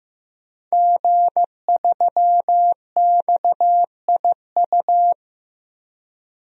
Morse code, heard as a keyed tone, about 15 wpm, 700 Hz. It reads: G3XIU
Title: DXCC-100 Call Signs - 5 Characters - Repeated 2x - 15wpm